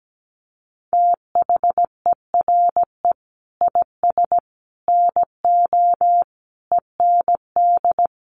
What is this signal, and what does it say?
Morse code, heard as a keyed tone, about 17 wpm, 700 Hz. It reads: THERE IS NO END
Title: Sentences from Top 1200 Words 17wpm